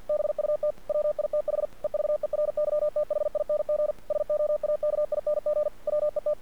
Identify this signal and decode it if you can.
Morse code, heard as a keyed tone, about 48 wpm, 600 Hz. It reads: BUT WITH EVERYTHING SOAKING WET
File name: archived / CRSnet